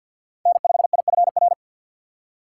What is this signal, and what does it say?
Morse code, heard as a keyed tone, about 52 wpm, 700 Hz. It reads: N5IFR
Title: USA Call Signs - 5 Characters - 2x Character Spacing - 50wpm